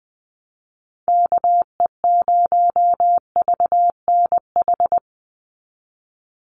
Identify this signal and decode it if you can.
Morse code, heard as a keyed tone, about 20 wpm, 700 Hz. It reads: KE0VNH
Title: USA Call Signs 20wpm